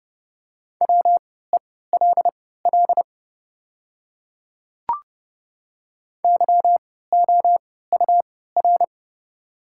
Morse code, heard as a keyed tone, about 30 wpm, spaced as for 17 wpm, 700 Hz. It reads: WELL YOUR
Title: 3x Character Spacing - Top 200 Words - 30wpm